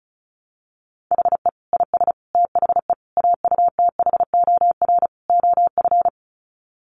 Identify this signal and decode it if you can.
Morse code, heard as a keyed tone, about 34 wpm, 700 Hz. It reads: HE IS THE AUTHOR OF
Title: Sets of 5 Words Encore2 35wpm